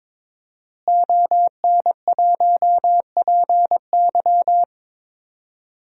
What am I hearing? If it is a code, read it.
Morse code, heard as a keyed tone, about 21 wpm, 700 Hz. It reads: ON1PY
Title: DXCC-100 Call Signs - 5 Characters - Repeated 2x - 22wpm